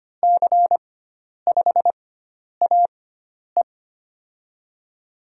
Morse code, heard as a keyed tone, about 25 wpm, 700 Hz. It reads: C 5 A E
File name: Lesson 16g - Letter c - Call Signs 25wpm